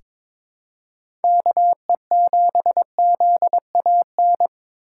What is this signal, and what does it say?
Morse code, heard as a keyed tone, about 21 wpm, 700 Hz. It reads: KE7ZAN
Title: USA Call Signs - 6 Characters - 3x Character Spacing - 22wpm